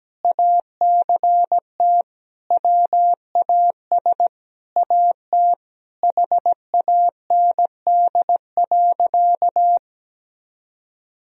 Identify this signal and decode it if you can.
Morse code, heard as a keyed tone, about 17 wpm, 700 Hz. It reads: ACT WAS AT HAND.
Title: Sentences from Top 1000 Words - Review Part 2 - 17wpm